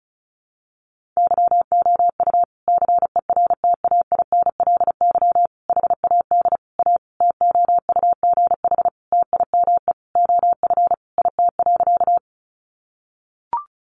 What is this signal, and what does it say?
Morse code, heard as a keyed tone, about 34 wpm, 700 Hz. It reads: YOU CERTAINLY HAD A TOUGH TIME OF IT.
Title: Sentences from Top 1200 Words 35wpm